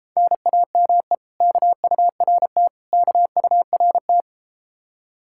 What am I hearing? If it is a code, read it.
Morse code, heard as a keyed tone, about 32 wpm, 700 Hz. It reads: NAME KURT KURT